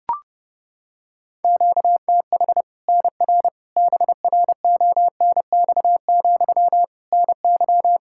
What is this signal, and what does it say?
Morse code, heard as a keyed tone, about 31 wpm, 700 Hz. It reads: QTH NR BRONX, NY
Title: Practice QSO - Exchange 1 - Beginning - 30wpm